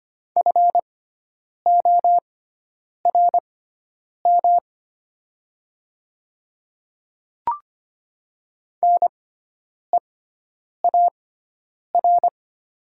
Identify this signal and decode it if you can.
Morse code, heard as a keyed tone, about 25 wpm, spaced as for 9 wpm, 700 Hz. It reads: FORM NEAR
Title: 6x Character Spacing - Top 200 Words - 25wpm